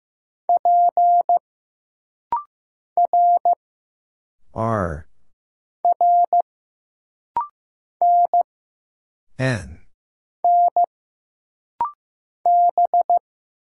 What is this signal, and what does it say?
Morse code, heard as a keyed tone, about 15 wpm, 700 Hz. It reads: P R R N N B
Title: Single Letters - Rapid-Fire 15wpm